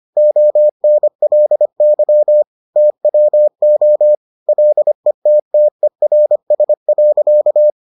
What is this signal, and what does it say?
Morse code, heard as a keyed tone, about 25 wpm, 600 Hz. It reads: ONLY TWO LETTERS.